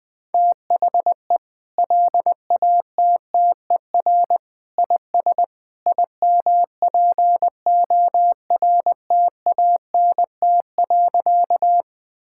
Morse code, heard as a keyed tone, about 21 wpm, 700 Hz. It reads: THE LATTER IS IMPORTANT.
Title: Sentences from Top 1600 Words 20wpm